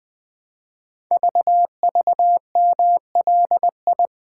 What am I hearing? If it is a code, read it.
Morse code, heard as a keyed tone, about 21 wpm, 700 Hz. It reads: VVMLI
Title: Code Groups - 5 Letters - 20wpm